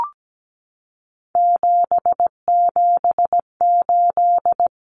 Morse code, heard as a keyed tone, about 17 wpm, 700 Hz. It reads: 778
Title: Common Cut Numbers - 3-Digit 17wpm